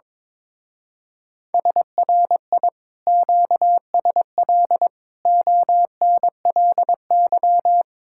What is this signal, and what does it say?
Morse code, heard as a keyed tone, about 22 wpm, 700 Hz. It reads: SRI QSL ONLY
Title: Speed-Racing Common QSO Phrases - 22wpm